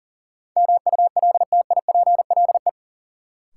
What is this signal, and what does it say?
Morse code, heard as a keyed tone, about 40 wpm, 700 Hz. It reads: MULTIPLE
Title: Sets of 5 Words - New Word Drill - 40wpm